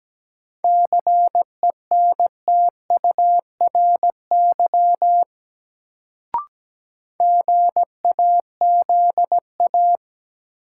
Morse code, heard as a keyed tone, about 17 wpm, 700 Hz. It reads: CENTURY GAZA
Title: Sets of 5 Words - New Word Drill - 17wpm